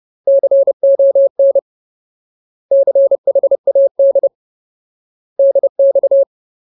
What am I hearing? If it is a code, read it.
Morse code, heard as a keyed tone, about 30 wpm, 550 Hz. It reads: CON CHAD DX
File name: Ultimate Word List / 30WPM Ultimate